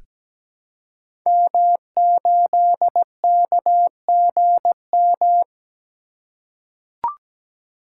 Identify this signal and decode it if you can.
Morse code, heard as a keyed tone, about 17 wpm, 700 Hz. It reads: M8KGM
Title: Code Groups - 5 Letter:Numbers - 17wpm